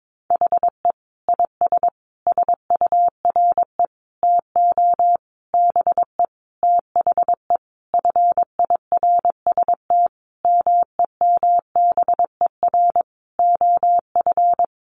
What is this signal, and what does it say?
Morse code, heard as a keyed tone, about 22 wpm, 700 Hz. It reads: HE IS SURE TO BE THE FIRST MEMBER OF
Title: Sentences from Top 1400 Words 22wpm